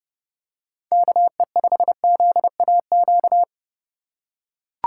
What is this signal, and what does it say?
Morse code, heard as a keyed tone, about 31 wpm, 690 Hz. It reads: KE5ZAQ
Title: USA Call Signs - 3x Character Spacing - 30wpm